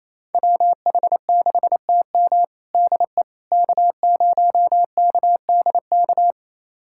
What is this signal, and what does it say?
Morse code, heard as a keyed tone, about 28 wpm, 700 Hz. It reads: WH6TM DE K0KDK
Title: USA Callsign DE Callsign - 5 Characters - 28wpm